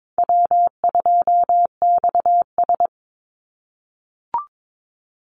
Morse code, heard as a keyed tone, about 22 wpm, 700 Hz. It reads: W2XS
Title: Easy Call Signs - 22wpm